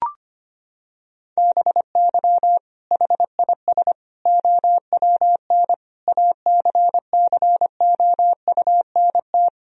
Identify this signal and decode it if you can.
Morse code, heard as a keyed tone, about 25 wpm, 700 Hz. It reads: BY HIS OWN ACCOUNT
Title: Sentences from Top 1000 Words - Review Part 1 - 25wpm